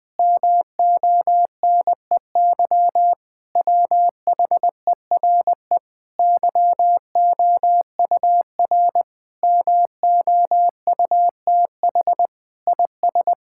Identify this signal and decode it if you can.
Morse code, heard as a keyed tone, about 20 wpm, 700 Hz. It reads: MONEY WHERE YOUR MOUTH IS
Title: Sentences with 8-Word Phrases - 20wpm